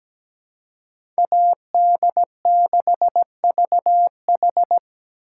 Morse code, heard as a keyed tone, about 20 wpm, 700 Hz. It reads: AD6VH
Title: Speed Racing - USA Call Signs 17wpm